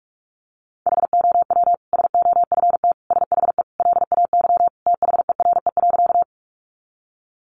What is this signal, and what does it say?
Morse code, heard as a keyed tone, about 44 wpm, 700 Hz. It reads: HOW SOFT SHE LAY THERE.
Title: Sentences from Top 1500 Words 45wpm